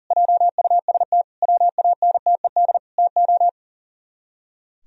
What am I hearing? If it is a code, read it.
Morse code, heard as a keyed tone, about 43 wpm, 700 Hz. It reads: JUST WANTED TO
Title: Speed-Racing - Sets of 4 Words 40wpm